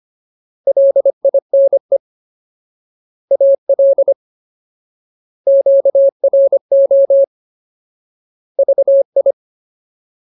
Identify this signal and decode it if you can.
Morse code, heard as a keyed tone, about 25 wpm, 550 Hz. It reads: LINE AL QRO VI
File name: Ultimate Word List / 25WPM Ultimate